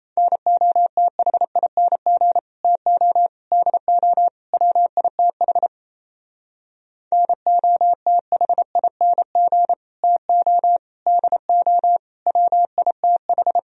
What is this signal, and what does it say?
Morse code, heard as a keyed tone, about 35 wpm, 700 Hz. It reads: NOTHING TO DO WITH NOTHING TO DO WITH